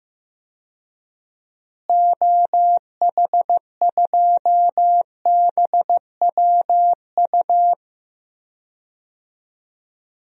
Morse code, heard as a keyed tone, about 15 wpm, 700 Hz. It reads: OH2BWU
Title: DXCC-100 Call Signs - 6 Characters - Repeated 2x - 15wpm